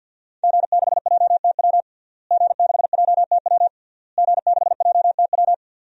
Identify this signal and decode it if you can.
Morse code, heard as a keyed tone, about 52 wpm, 700 Hz. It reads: G6JTW G6JTW G6JTW
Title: DXCC-100 Call Signs - 5 Characters - Repeated 3x - 50wpm